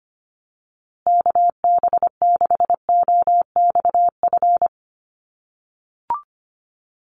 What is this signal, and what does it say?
Morse code, heard as a keyed tone, about 25 wpm, 700 Hz. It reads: KB6OXF
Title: USA Call Signs Repeated 2x 25wpm